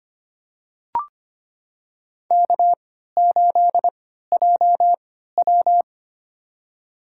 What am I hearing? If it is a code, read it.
Morse code, heard as a keyed tone, about 25 wpm, spaced as for 14 wpm, 700 Hz. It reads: K8JW
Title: USA Call Signs - 3x Character Spacing - 25wpm